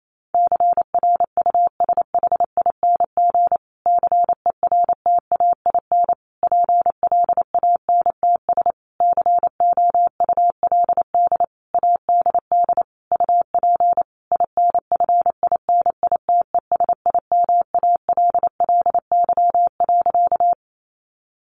Morse code, heard as a keyed tone, about 28 wpm, 700 Hz. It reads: CRUSHING CERTAIN PLANTS COULD ADD UP INFINITESIMALLY.
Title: A Sound Of Thunder 28wpm